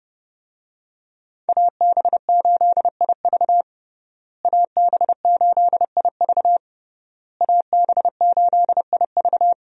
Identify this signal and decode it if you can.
Morse code, heard as a keyed tone, about 32 wpm, 700 Hz. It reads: AB8IV AB8IV AB8IV
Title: USA Call Signs - 5 Characters - Repeated 3x - 30wpm